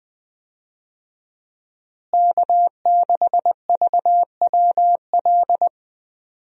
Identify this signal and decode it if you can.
Morse code, heard as a keyed tone, about 20 wpm, 700 Hz. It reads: K6VWL
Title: USA Call Signs - 5 Characters - Repeated 2x - 20wpm